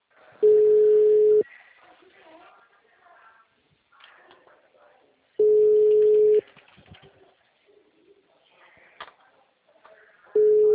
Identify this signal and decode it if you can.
Ringback tone (Europe, 425 Hz).
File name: out- -2008- - - .2110